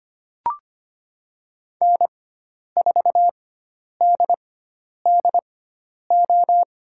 Morse code, heard as a keyed tone, about 25 wpm, spaced as for 10 wpm, 700 Hz. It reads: N4DDO
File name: Lesson 16g - Letter c - Call Signs 25wpm